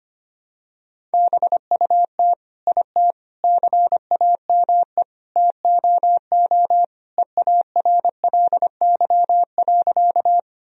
Morse code, heard as a keyed tone, about 25 wpm, 700 Hz. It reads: BUT IT CAME TOO EARLY.
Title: Sentences from Top 1000 Words - Review Part 2 - 25wpm